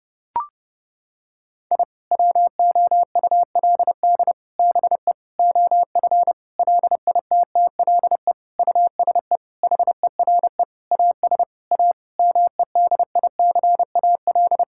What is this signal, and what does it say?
Morse code, heard as a keyed tone, about 30 wpm, 700 Hz. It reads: I WOULD BE OF LITTLE USE HERE AS A MEDICAL
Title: Sentences from Top 1000 Words - Review Part 5 - 30wpm